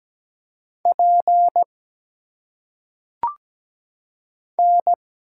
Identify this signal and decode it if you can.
Morse code, heard as a keyed tone, about 17 wpm, 700 Hz. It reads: P N
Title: Single Letters - Phonetic Pronunciation 17wpm